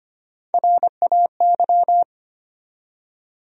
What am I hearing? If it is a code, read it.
Morse code, heard as a keyed tone, about 25 wpm, 700 Hz. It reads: RAY
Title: USA Common Names - Max 4 Letters - 25wpm